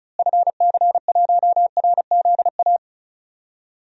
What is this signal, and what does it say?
Morse code, heard as a keyed tone, about 36 wpm, 700 Hz. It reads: FC1RZA
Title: DXCC-100 Call Signs - 6 Characters - Repeated 2x - 35wpm